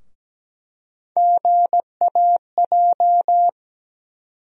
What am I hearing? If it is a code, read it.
Morse code, heard as a keyed tone, about 17 wpm, 700 Hz. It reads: GAJ
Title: USA Call Sign Postfixes - 3 Letters 17wpm